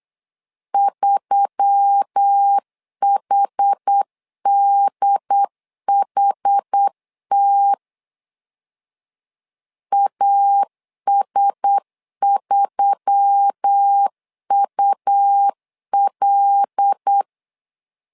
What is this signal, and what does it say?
Morse code, heard as a keyed tone, about 8 wpm, 800 Hz. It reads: 3HDHT AS3UL
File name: LEZIONI / 5 LBVC / 5 LB 8wpm 2816